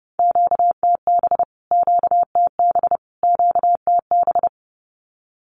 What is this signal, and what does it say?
Morse code, heard as a keyed tone, about 30 wpm, 700 Hz. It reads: QTB QTB QTB
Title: Q-Codes Repeated 3x 30wpm